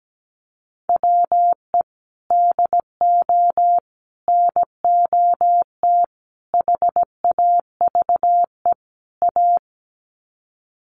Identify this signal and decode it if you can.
Morse code, heard as a keyed tone, about 17 wpm, 700 Hz. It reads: WE DO NOT HAVE A
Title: Sets of 5 Words 17wpm